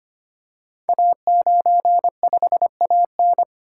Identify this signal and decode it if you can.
Morse code, heard as a keyed tone, about 25 wpm, 690 Hz. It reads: A95AN
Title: Code Groups - 5 Letter:Numbers - 25wpm